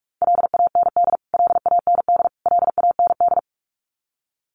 Morse code, heard as a keyed tone, about 44 wpm, 700 Hz. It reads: LAND LAND LAND
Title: Top 200 Words Repeated 45wpm